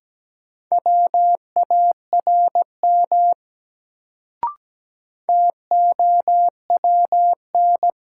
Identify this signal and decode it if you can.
Morse code, heard as a keyed tone, about 17 wpm, 700 Hz. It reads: WARM TOWN
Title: Sentences from Top 500 Words - New Word Drill - 17wpm